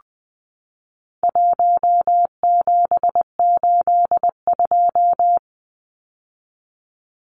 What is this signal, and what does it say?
Morse code, heard as a keyed tone, about 20 wpm, 700 Hz. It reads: 1782
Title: Numbers - 4-Digit 20wpm